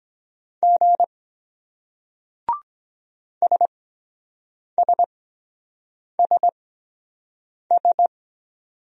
Morse code, heard as a keyed tone, about 23 wpm, spaced as for 7 wpm, 700 Hz. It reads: G SSSS